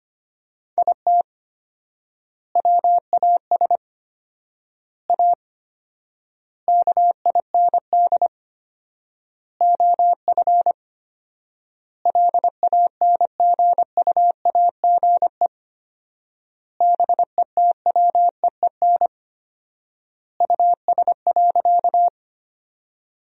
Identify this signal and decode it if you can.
Morse code, heard as a keyed tone, about 25 wpm, 700 Hz. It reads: IT WAS A KIND OF LANGUAGE BETWEEN US.
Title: 4x Word-Spacing - Sentences from Top 500 Words - 25wpm